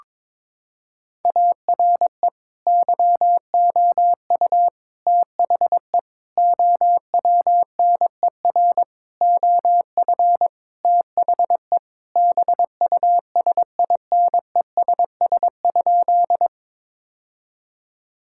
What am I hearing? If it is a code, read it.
Morse code, heard as a keyed tone, about 22 wpm, 700 Hz. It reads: ARE YOU THE OWNER OF THE BUSINESS?
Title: Sentences from Top 1600 Words 22wpm